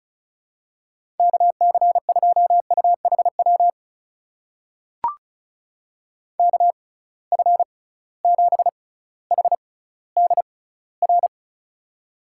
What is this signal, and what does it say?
Morse code, heard as a keyed tone, about 34 wpm, spaced as for 18 wpm, 700 Hz. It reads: KC2UHW KF7HDR
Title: USA Call Signs - 6 Characters - 6x Character Spacing - 35wpm